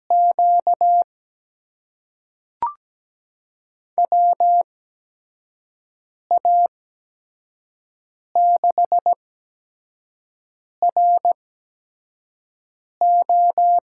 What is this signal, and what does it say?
Morse code, heard as a keyed tone, about 17 wpm, spaced as for 5 wpm, 700 Hz. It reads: Q WA6RO
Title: USA Call Signs - 5 Characters - 8x Character Spacing - 17wpm